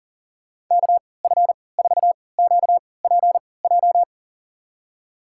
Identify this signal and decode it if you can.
Morse code, heard as a keyed tone, about 38 wpm, spaced as for 22 wpm, 700 Hz. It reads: KF4QPJ
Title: USA Call Signs - 6 Characters - 3x Character Spacing - 40wpm